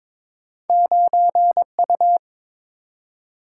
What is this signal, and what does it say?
Morse code, heard as a keyed tone, about 22 wpm, 700 Hz. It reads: 9U
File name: practice / 22wpm / Two Letter-Number Combinations - 22wpm